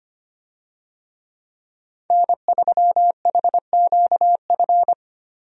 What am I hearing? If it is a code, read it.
Morse code, heard as a keyed tone, about 25 wpm, 700 Hz. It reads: N3HQF
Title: USA Call Signs - 5 Characters - Repeated 2x - 25wpm